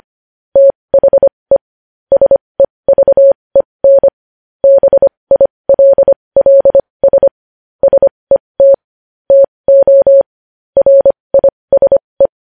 Morse code, heard as a keyed tone, about 25 wpm, spaced as for 20 wpm, 550 Hz. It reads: THE SEVEN BILLS SET TO RISE
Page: news headlines in morse code at 20 wpm